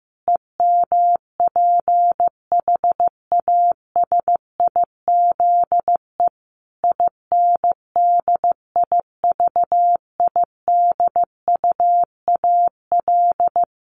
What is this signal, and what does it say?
Morse code, heard as a keyed tone, about 15 wpm, 700 Hz. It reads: EMPHASIZE INDIVIDUAL
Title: Sentences from Top 1500 Words 15wpm